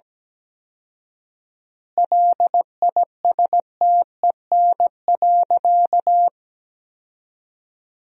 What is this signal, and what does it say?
Morse code, heard as a keyed tone, about 17 wpm, 700 Hz. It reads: LISTEN.
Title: 4x Word-Spacing - Sentences from Top 500 Words - 17wpm